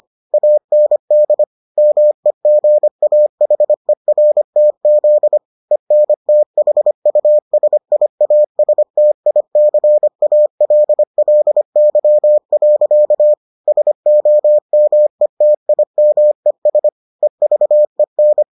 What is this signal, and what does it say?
Morse code, heard as a keyed tone, about 25 wpm, 600 Hz. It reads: AND MEGAHERTZ ENTHUSIASTICALLY. SOMETIMES EVEN